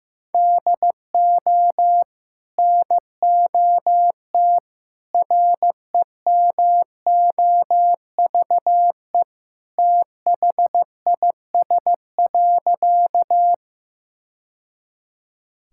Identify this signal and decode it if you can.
Morse code, heard as a keyed tone, about 15 wpm, 700 Hz. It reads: DO NOT REMOVE THIS.
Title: Sentences from Top 1000 Words - Review Part 5 - 15wpm